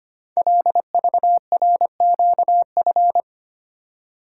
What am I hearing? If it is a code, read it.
Morse code, heard as a keyed tone, about 25 wpm, 700 Hz. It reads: LVRQF
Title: Code Groups - 5 Letters - 25wpm